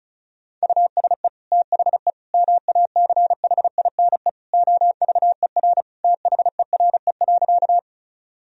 Morse code, heard as a keyed tone, about 36 wpm, 700 Hz. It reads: USE THE MACHINE OVER THERE.
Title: Sentences from Top 1000 Words - Review Part 2 - 35wpm